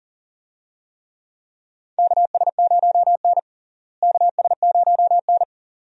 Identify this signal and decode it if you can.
Morse code, heard as a keyed tone, about 40 wpm, 700 Hz. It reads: KS0N KS0N
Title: USA Call Signs - 4 Characters - Repeated 2x - 40wpm